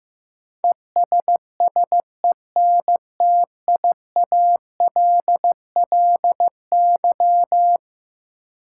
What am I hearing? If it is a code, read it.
Morse code, heard as a keyed tone, about 15 wpm, 700 Hz. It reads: ESSENTIALLY
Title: Sentences from Top 1300 Words - New Word Drill - 15wpm